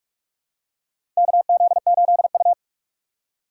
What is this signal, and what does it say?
Morse code, heard as a keyed tone, about 44 wpm, 700 Hz. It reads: KZ8U
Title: USA Call Signs - 4 Characters - 6x Character Spacing - 45wpm